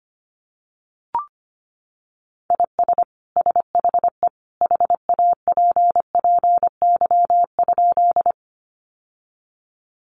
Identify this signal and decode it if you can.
Morse code, heard as a keyed tone, about 25 wpm, 700 Hz. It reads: IS SHE HAPPY?
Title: Sentences from Top 1000 Words 25wpm